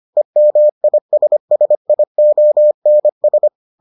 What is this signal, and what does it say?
Morse code, heard as a keyed tone, about 25 wpm, 600 Hz. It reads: EMISSIONS